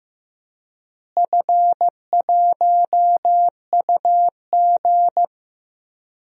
Morse code, heard as a keyed tone, about 15 wpm, 700 Hz. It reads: F1UG
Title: DXCC-100 Call Signs - 4 Characters - Repeated 3x - 15wpm